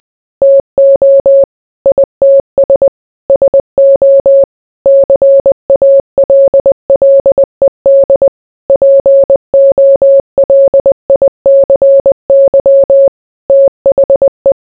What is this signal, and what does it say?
Morse code, heard as a keyed tone, about 20 wpm, 550 Hz. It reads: TO ITS SO CALLED POLICY THE